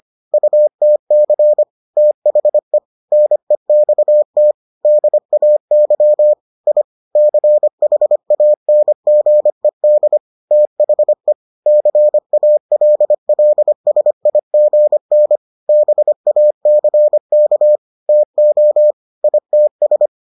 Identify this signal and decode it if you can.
Morse code, heard as a keyed tone, about 25 wpm, 600 Hz. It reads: UTC THE NEXT DAY I CHANGED THE CALLSIGN BACK TO ITS